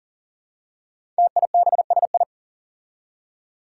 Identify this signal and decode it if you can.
Morse code, heard as a keyed tone, about 40 wpm, 700 Hz. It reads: TIBSI
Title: Five Challenging Letter-Number Combinations 40wpm